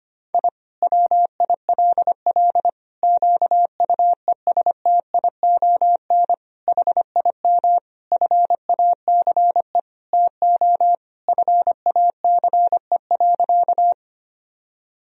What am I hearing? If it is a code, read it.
Morse code, heard as a keyed tone, about 25 wpm, 700 Hz. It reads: I WILL QUESTION HIM FACE TO FACE.
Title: Sentences from Top 1000 Words - Review Part 2 - 25wpm